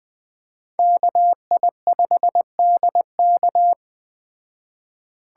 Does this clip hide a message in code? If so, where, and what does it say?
Yes: Morse code from 0.8 to 3.7 s, heard as a keyed tone, about 20 wpm, 700 Hz. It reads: KI5DK